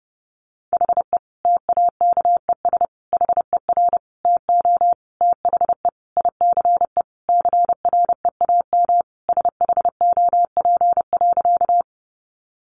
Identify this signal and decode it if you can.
Morse code, heard as a keyed tone, about 30 wpm, 700 Hz. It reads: HE TAKES HER TO THE ICE CREAM SHOP.
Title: Additional Easy Phrases 30wpm